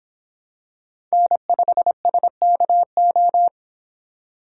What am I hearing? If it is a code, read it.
Morse code, heard as a keyed tone, about 25 wpm, 700 Hz. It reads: N5SKO